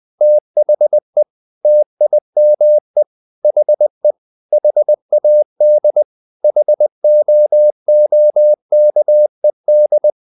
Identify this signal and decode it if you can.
Morse code, heard as a keyed tone, about 20 wpm, 600 Hz. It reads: THE TIME HE HAD HOOKED
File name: the old man and the sea / the old man and the sea part 1 20wpm